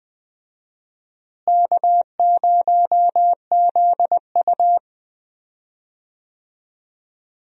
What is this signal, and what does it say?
Morse code, heard as a keyed tone, about 21 wpm, 700 Hz. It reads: K0ZU
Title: USA Call Signs - 4 Characters - Repeated 2x - 20wpm